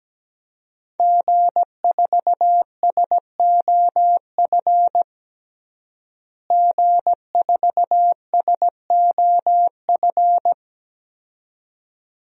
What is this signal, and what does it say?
Morse code, heard as a keyed tone, about 17 wpm, 700 Hz. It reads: G4SOF G4SOF
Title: DXCC-100 Call Signs - 5 Characters - Repeated 2x - 17wpm